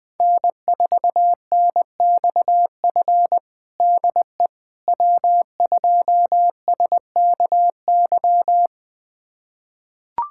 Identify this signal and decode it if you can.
Morse code, heard as a keyed tone, about 20 wpm, 700 Hz. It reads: N4NXF DE W2SKY
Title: USA Callsign DE Callsign - 5 Characters - 20wpm